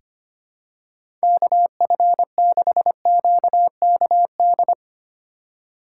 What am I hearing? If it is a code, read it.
Morse code, heard as a keyed tone, about 25 wpm, 700 Hz. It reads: KF6QKD
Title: USA Call Signs - 6 Characters - 2x Character Spacing - 25wpm